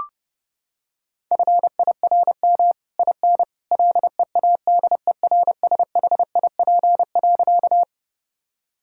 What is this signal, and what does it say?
Morse code, heard as a keyed tone, about 30 wpm, 700 Hz. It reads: FIRM IN LEADERSHIP.
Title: Sentences from Top 1600 Words 30wpm